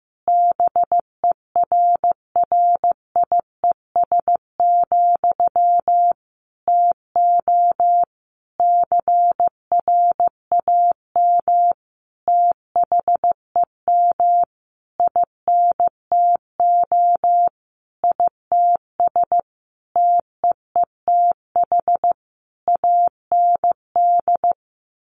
Morse code, heard as a keyed tone, about 15 wpm, 700 Hz. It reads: BERRIES, TO CRAM THEM INTO ITS TEETH AND
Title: A Sound Of Thunder 15wpm